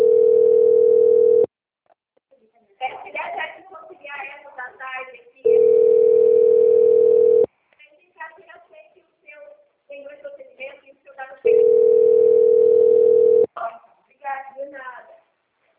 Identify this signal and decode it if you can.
Ringback tone (North America), 440 + 480 Hz, with speech.
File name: out- - - - - .11302